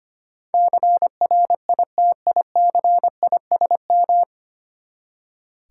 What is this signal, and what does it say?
Morse code, heard as a keyed tone, about 25 wpm, 700 Hz. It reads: CRITICISM
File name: Nine Letter Words - 25wpm